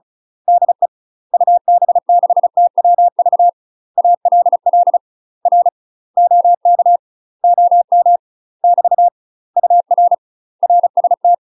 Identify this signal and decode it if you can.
Morse code, heard as a keyed tone, about 36 wpm, 700 Hz. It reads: DE UB6TWV ALL R OK OM <BT> UR RST
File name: UK MASTER QSOs TRAINING/35WPM